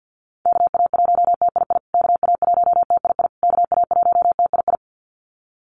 Morse code, heard as a keyed tone, about 49 wpm, 700 Hz. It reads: KA1TII KA1TII KA1TII
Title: USA Call Signs Repeated 3x 50wpm